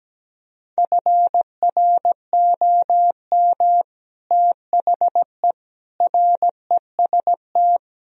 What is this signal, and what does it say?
Morse code, heard as a keyed tone, about 17 wpm, 700 Hz. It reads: FROM THE REST
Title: Speed-Racing - Sets of 5 Words 17wpm